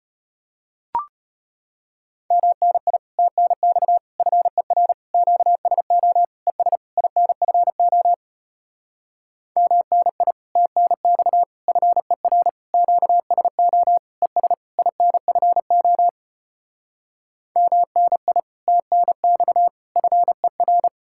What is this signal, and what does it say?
Morse code, heard as a keyed tone, about 33 wpm, 700 Hz. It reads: MNI TNX FER QSO ES INFO MNI TNX FER QSO ES INFO MNI TNX FER